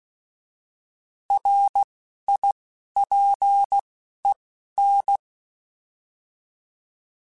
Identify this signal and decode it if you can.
Morse code, heard as a keyed tone, about 16 wpm, 800 Hz. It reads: R I P E N